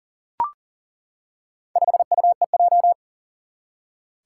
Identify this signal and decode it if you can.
Morse code, heard as a keyed tone, about 40 wpm, 700 Hz. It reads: 5UEJ
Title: Four Challenging Letter-Number Combinations 40wpm